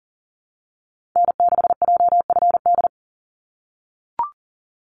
Morse code, heard as a keyed tone, about 40 wpm, 700 Hz. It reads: N6JFD
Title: NA SOTA Activator Callsigns 40wpm